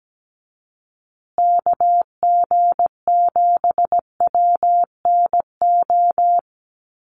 Morse code, heard as a keyed tone, about 17 wpm, 700 Hz. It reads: KG7WNO
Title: USA Call Signs Repeated 2x 17wpm